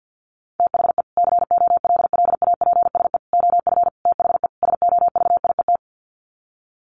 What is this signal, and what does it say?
Morse code, heard as a keyed tone, about 50 wpm, 700 Hz. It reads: THE COLLAPSE OF THE SOVIET
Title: Sets of 5 Words Encore 50wpm